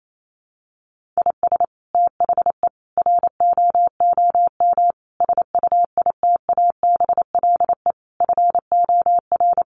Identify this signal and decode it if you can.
Morse code, heard as a keyed tone, about 28 wpm, 700 Hz. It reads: IS THE ROOM SUITABLE FOR
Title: Sentences from Top 1400 Words 28wpm